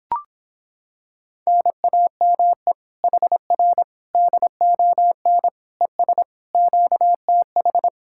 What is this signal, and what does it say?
Morse code, heard as a keyed tone, about 26 wpm, 700 Hz. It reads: NAME HR DON ES QTH